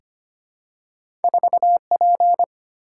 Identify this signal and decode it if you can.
Morse code, heard as a keyed tone, about 25 wpm, 690 Hz. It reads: 4P
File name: Two Letter-Number Combinations - 25wpm